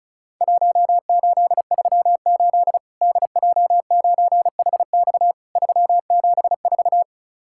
Morse code, heard as a keyed tone, about 34 wpm, 700 Hz. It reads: 1838 DJ9HX 374
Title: Worked All Europe DX Contest - QTC Exchange - 35wpm